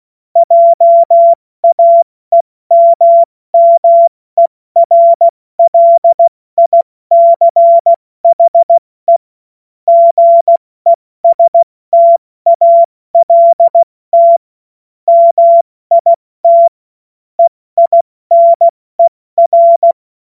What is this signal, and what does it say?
Morse code, heard as a keyed tone, about 16 wpm, spaced as for 14 wpm, 680 Hz. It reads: JAEMMERLICHE GESTALT MIT EINER
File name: Witz-16-September-2025-70-F80